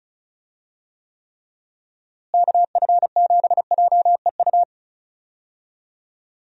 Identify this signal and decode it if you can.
Morse code, heard as a keyed tone, about 36 wpm, 700 Hz. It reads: KF7JEU
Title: USA Call Signs - 6 Characters - Repeated 3x - 35wpm